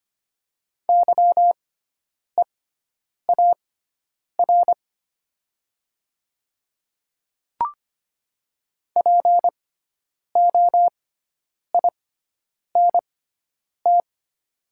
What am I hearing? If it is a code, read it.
Morse code, heard as a keyed tone, about 25 wpm, spaced as for 9 wpm, 700 Hz. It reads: YEAR POINT
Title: 6x Character Spacing - Top 500 Words - 25wpm